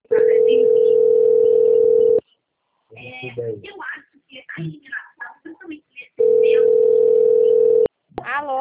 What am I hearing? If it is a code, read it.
Ringback tone (North America), 440 + 480 Hz, with speech.
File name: out- - - - - .13314